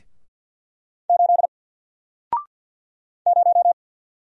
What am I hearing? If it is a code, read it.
Morse code, heard as a keyed tone, about 50 wpm, 700 Hz. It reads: 8 0
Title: Single Letter-Number 50wpm